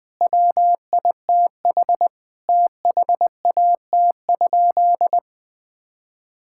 Morse code, heard as a keyed tone, about 20 wpm, 700 Hz. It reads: WITH THAT?
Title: Sentences from Top 1000 Words - Review Part 6 - 20wpm